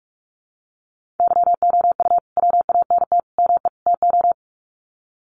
Morse code, heard as a keyed tone, about 44 wpm, 700 Hz. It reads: YOU WANT ME TO
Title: Sets of 4 Words 45wpm